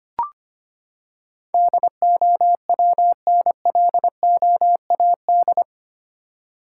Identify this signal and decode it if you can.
Morse code, heard as a keyed tone, about 25 wpm, 700 Hz. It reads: DOWNLOAD
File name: Eight Letter Words - 25wpm